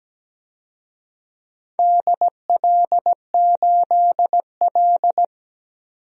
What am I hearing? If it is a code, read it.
Morse code, heard as a keyed tone, about 17 wpm, 700 Hz. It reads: DL8L
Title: DXCC-100 Call Signs - 4 Characters - 17wpm